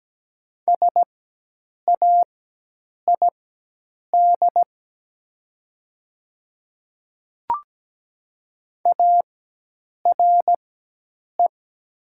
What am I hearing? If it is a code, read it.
Morse code, heard as a keyed tone, about 17 wpm, spaced as for 8 wpm, 700 Hz. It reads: SAID ARE
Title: 4x Character Spacing - Top 200 Words - 17wpm